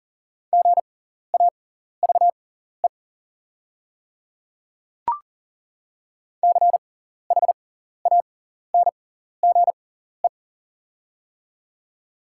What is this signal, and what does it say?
Morse code, heard as a keyed tone, about 40 wpm, spaced as for 14 wpm, 690 Hz. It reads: GAVE CHANGE
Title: 6x Character Spacing - Top 500 Words - 40wpm